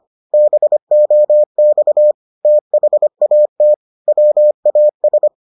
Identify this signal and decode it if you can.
Morse code, heard as a keyed tone, about 25 wpm, 600 Hz. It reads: BOX THAT WAS